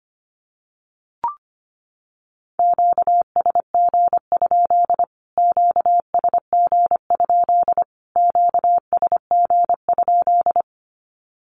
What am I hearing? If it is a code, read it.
Morse code, heard as a keyed tone, about 25 wpm, 700 Hz. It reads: QSG? QSG? QSG?
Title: Q-Codes Repeated 3x 25wpm